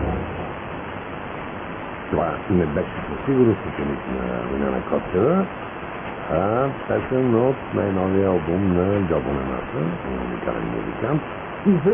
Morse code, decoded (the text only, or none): none